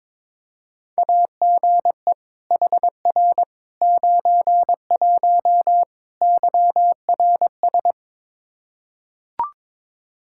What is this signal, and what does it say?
Morse code, heard as a keyed tone, about 22 wpm, 700 Hz. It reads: AGE HR 91 YRS
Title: Practice QSO - Exchange 3 - More Info - 22wpm